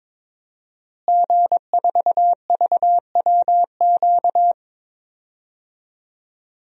Morse code, heard as a keyed tone, about 22 wpm, 700 Hz. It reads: G4VWQ
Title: DXCC-100 Call Signs - 5 Characters - Repeated 2x - 22wpm